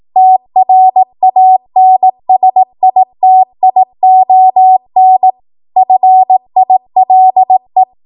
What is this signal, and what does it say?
Morse code, heard as a keyed tone, about 18 wpm, 750 Hz. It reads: TRANSITION FILE
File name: Morse / 18WPM